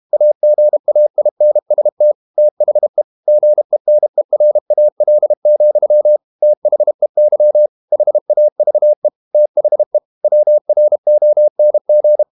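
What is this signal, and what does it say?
Morse code, heard as a keyed tone, about 32 wpm, 600 Hz. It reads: AGAINST THE GENERAL, THEY HAVE THE WRONG
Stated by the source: Morse code practice at 32 wpm from CW med Gnister, reading Ken Follett's novel Never in English.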